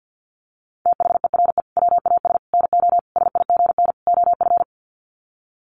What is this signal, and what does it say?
Morse code, heard as a keyed tone, about 50 wpm, 700 Hz. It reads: THERE WAS NO SIGN OF
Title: Sets of 5 Words Encore2 50wpm